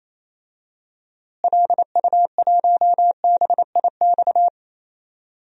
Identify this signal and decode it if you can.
Morse code, heard as a keyed tone, about 28 wpm, 700 Hz. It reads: LU1BIX
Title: DXCC-100 Call Signs - 6 Characters - Repeated 3x - 28wpm